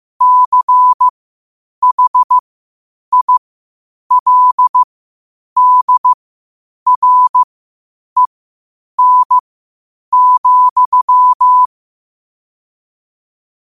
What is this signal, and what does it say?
Morse code, heard as a keyed tone, about 15 wpm, spaced as for 8 wpm, 1000 Hz. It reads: CHILDREN,
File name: 8wpm